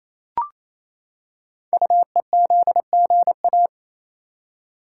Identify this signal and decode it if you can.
Morse code, heard as a keyed tone, about 27 wpm, 700 Hz. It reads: UEZGA
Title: Code Groups - 5 Letter:Numbers - 28wpm